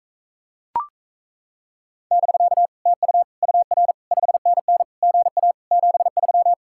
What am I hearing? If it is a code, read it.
Morse code, heard as a keyed tone, about 38 wpm, 700 Hz. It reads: <BK> TU UR 5NN GA 73